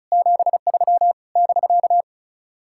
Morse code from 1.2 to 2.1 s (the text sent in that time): <BK>